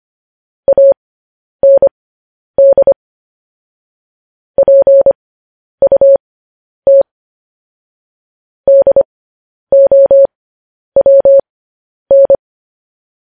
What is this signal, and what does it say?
Morse code, heard as a keyed tone, about 25 wpm, spaced as for 10 wpm, 550 Hz. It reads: AND PUT DOWN
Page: news headlines in morse code at 10 wpm